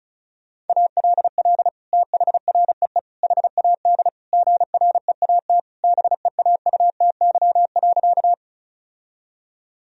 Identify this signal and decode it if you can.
Morse code, heard as a keyed tone, about 34 wpm, 700 Hz. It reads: ALL THREE HAD GREAT BEAUTY.
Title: Sentences from Top 1000 Words - Review Part 2 - 35wpm